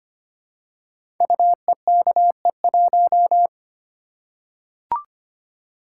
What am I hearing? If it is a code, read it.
Morse code, heard as a keyed tone, about 25 wpm, 700 Hz. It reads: UEKE1
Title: Code Groups - 5 Letter:Numbers - 25wpm